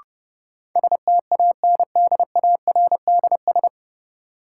Morse code, heard as a keyed tone, about 32 wpm, 700 Hz. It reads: STANDARDS
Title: Sentences from Top 1000 Words - Review Part 6 - New Word Drill - 30wpm